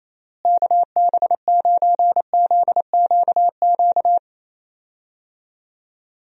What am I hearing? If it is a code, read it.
Morse code, heard as a keyed tone, about 27 wpm, 700 Hz. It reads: KB9ZQQ
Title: USA Call Signs - 6 Characters - Repeated 2x - 28wpm